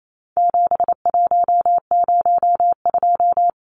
Morse code, heard as a keyed tone, about 28 wpm, 700 Hz. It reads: 7102
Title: Numbers - 4-Digit 28wpm